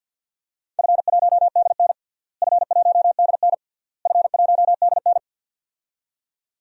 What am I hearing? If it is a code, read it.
Morse code, heard as a keyed tone, about 48 wpm, 700 Hz. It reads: F1DN F1DN F1DN
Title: DXCC-100 Call Signs - 4 Characters - Repeated 3x - 50wpm